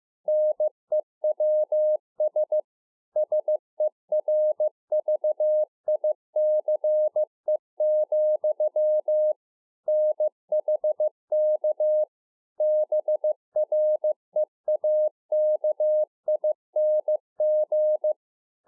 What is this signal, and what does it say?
Morse code, heard as a keyed tone, about 15 wpm, 610 Hz. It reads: NEWS SERVICE, NHK BREAKING
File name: 2011-04-mp3 / THE 3.11 6 15WPM